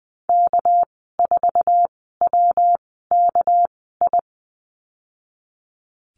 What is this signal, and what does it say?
Morse code, heard as a keyed tone, about 19 wpm, spaced as for 14 wpm, 700 Hz. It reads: K4WKI
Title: 200 Lesson 32g - Letter k - Call Signs 20wpm